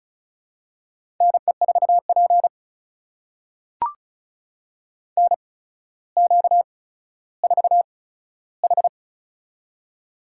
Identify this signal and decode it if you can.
Morse code, heard as a keyed tone, about 33 wpm, spaced as for 15 wpm, 700 Hz. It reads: NE4P NQ4H
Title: USA Call Signs - 4 Characters - 8x Character Spacing - 35wpm